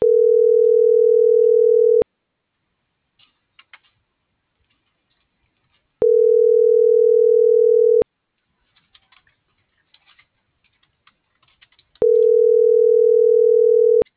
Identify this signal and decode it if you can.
Ringback tone (North America), 440 + 480 Hz.